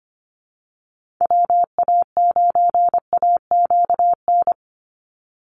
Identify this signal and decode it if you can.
Morse code, heard as a keyed tone, about 25 wpm, 700 Hz. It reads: WA9AQN
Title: Call Signs Repeated 2x - 25wpm